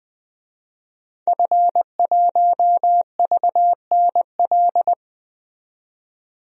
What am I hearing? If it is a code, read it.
Morse code, heard as a keyed tone, about 20 wpm, 700 Hz. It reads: F1VNL
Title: DXCC-100 Call Signs - 5 Characters - 20wpm